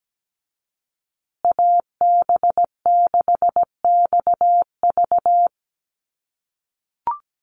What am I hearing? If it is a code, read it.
Morse code, heard as a keyed tone, about 17 wpm, 700 Hz. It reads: AB6XV
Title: USA Call Signs Repeated 2x 17wpm